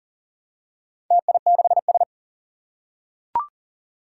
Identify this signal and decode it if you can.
Morse code, heard as a keyed tone, about 37 wpm, 700 Hz. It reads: TI6S
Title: Four Challenging Letter-Number Combinations 40wpm